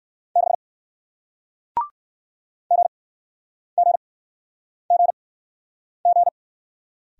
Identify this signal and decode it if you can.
Morse code, heard as a keyed tone, about 61 wpm, spaced as for 9 wpm, 700 Hz. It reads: 6 GGGG